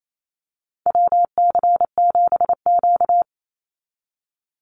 Morse code, heard as a keyed tone, about 28 wpm, 700 Hz. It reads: WC7Q
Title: Call Signs Repeated 2x - 28wpm